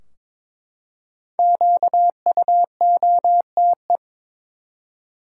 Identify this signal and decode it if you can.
Morse code, heard as a keyed tone, about 21 wpm, 700 Hz. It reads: QUOTE
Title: Five Letter Words - 22wpm